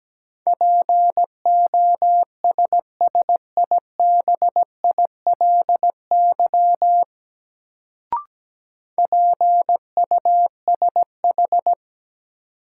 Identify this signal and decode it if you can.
Morse code, heard as a keyed tone, about 17 wpm, 700 Hz. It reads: POSSIBILY PUSH
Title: Sentences from Top 1300 Words - New Word Drill - 17wpm